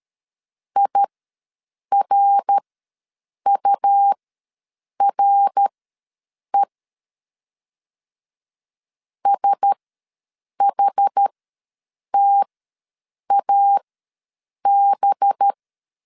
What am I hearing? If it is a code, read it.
Morse code, heard as a keyed tone, about 13 wpm, spaced as for 7 wpm, 800 Hz. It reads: IRURE SHTAB
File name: LEZIONI / 5 LBVC / 5 LBV ws 12wpm 3015